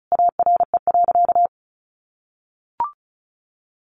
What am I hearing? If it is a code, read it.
Morse code, heard as a keyed tone, about 34 wpm, 700 Hz. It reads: ARE.
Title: Sentences from Top 700 Words 35wpm